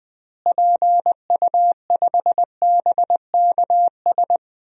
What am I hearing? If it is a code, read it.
Morse code, heard as a keyed tone, about 20 wpm, 700 Hz. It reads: PU5BKS
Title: DXCC-100 Call Signs - 6 Characters - Repeated 3x - 20wpm